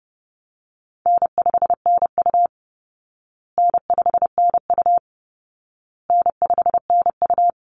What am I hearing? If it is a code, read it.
Morse code, heard as a keyed tone, about 30 wpm, 700 Hz. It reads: N5NU N5NU N5NU
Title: Easy Call Signs Repeated 3x - 30wpm